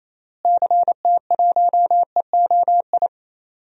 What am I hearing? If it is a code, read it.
Morse code, heard as a keyed tone, about 28 wpm, 700 Hz. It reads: CT1EOI
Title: DXCC-100 Call Signs - 6 Characters - Repeated 3x - 28wpm